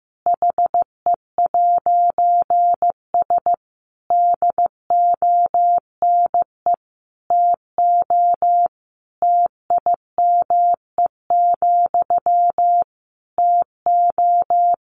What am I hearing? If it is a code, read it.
Morse code, heard as a keyed tone, about 15 wpm, 700 Hz. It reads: HE'S DONE TO TIME, TO
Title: A Sound Of Thunder 15wpm